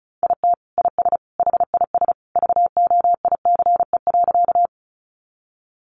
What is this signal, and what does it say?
Morse code, heard as a keyed tone, about 34 wpm, 700 Hz. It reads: IT IS HIS VOICE.
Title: Sentences from Top 500 Words 35wpm